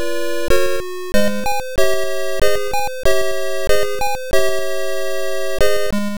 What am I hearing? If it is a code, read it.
There is no code.